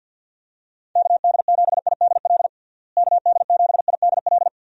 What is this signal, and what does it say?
Morse code, heard as a keyed tone, about 51 wpm, 700 Hz. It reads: KD7IDL KD7IDL
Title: USA Call Signs - 6 Characters - Repeated 2x - 50wpm